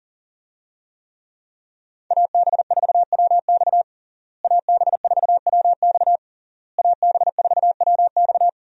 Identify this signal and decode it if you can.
Morse code, heard as a keyed tone, about 40 wpm, 700 Hz. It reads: AB4WX AB4WX AB4WX
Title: USA Call Signs - 5 Characters - Repeated 3x - 40wpm